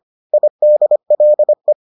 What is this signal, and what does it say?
Morse code, heard as a keyed tone, about 25 wpm, 600 Hz. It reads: IDLE